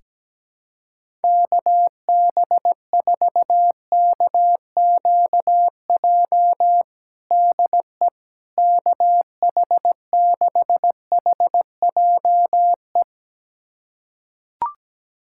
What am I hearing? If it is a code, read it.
Morse code, heard as a keyed tone, about 17 wpm, 700 Hz. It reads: KB4KQJ DE KH6HJE
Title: USA Callsign DE Callsign 17wpm